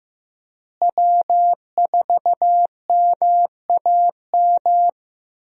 Morse code, heard as a keyed tone, about 15 wpm, 700 Hz. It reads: W4MAM
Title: USA Call Signs - 5 Characters - Repeated 2x - 15wpm